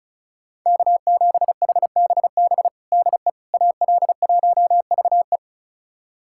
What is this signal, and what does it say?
Morse code, heard as a keyed tone, about 36 wpm, 700 Hz. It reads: K7HBB DE AL1VE
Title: USA Callsign DE Callsign - 5 Characters - 35wpm